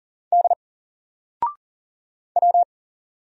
Morse code, heard as a keyed tone, about 40 wpm, 700 Hz. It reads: D W
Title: Single Letter-Number Alt 40wpm